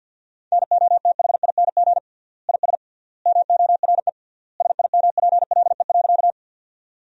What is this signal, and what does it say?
Morse code, heard as a keyed tone, about 53 wpm, 700 Hz. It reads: NOTHING IS MORE SIMPLE.
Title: 3x Word-Spacing - Sentences from Top 500 Words - 50wpm